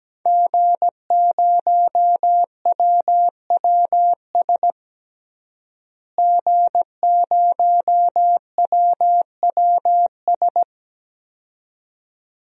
Morse code, heard as a keyed tone, about 17 wpm, 700 Hz. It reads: G0WWS G0WWS
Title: DXCC-100 Call Signs - 5 Characters - Repeated 2x - 17wpm